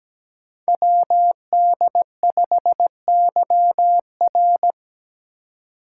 Morse code, heard as a keyed tone, about 17 wpm, 700 Hz. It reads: WD5YR
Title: Code Groups - 5 Letter:Numbers - 17wpm